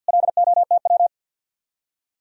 Morse code, heard as a keyed tone, about 66 wpm, spaced as for 57 wpm, 700 Hz. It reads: LOTW
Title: Speed-Racing Common QSO Phrases Encore - 50wpm